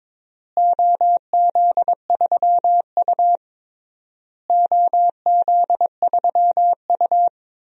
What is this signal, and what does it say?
Morse code, heard as a keyed tone, about 22 wpm, 700 Hz. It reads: OZ3U OZ3U
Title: DXCC-100 Call Signs - 4 Characters - Repeated 3x - 22wpm